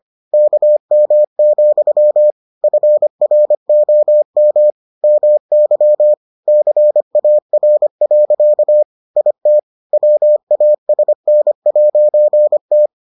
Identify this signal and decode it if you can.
Morse code, heard as a keyed tone, about 25 wpm, 600 Hz. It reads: KM, FROM MY CAR. IT WASN'T